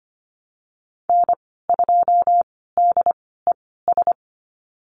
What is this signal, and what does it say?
Morse code, heard as a keyed tone, about 25 wpm, spaced as for 16 wpm, 700 Hz. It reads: N2DES
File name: Lesson 15g - Number 5 - Call Signs 25wpm